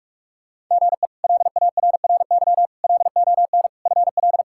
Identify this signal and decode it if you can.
Morse code, heard as a keyed tone, about 42 wpm, 700 Hz. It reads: GE LARRY LON FL
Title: K1USN SST QSO Practice - 45wpm